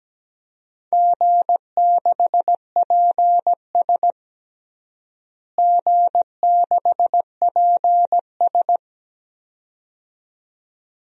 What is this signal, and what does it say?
Morse code, heard as a keyed tone, about 17 wpm, 700 Hz. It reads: G6PS G6PS
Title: DXCC-100 Call Signs - 4 Characters - Repeated 2x - 17wpm